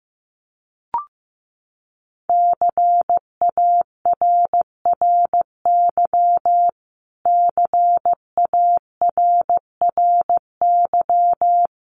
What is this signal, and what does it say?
Morse code, heard as a keyed tone, about 14 wpm, 700 Hz. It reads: CARRY CARRY
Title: Top 300 Words Repeated 15wpm